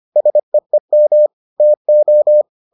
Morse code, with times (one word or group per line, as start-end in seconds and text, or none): 0.2-1.3 SEEM
1.6-2.4 TO